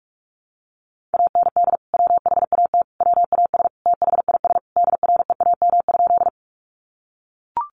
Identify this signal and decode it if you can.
Morse code, heard as a keyed tone, about 44 wpm, 700 Hz. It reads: AND WHAT WAS THIS DREAM?
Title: Sentences from Top 1500 Words 45wpm